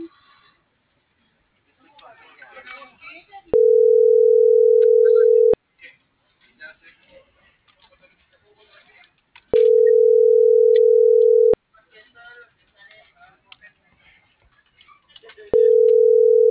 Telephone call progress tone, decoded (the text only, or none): ringback tone